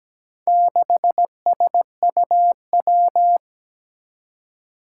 Morse code, heard as a keyed tone, about 17 wpm, 700 Hz. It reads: 6SUW
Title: Four Challenging Letter-Number Combinations 17wpm